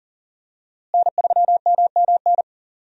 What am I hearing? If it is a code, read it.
Morse code, heard as a keyed tone, about 40 wpm, 700 Hz. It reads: N3MMN
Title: USA Call Signs - 5 Characters - Repeated 3x - 40wpm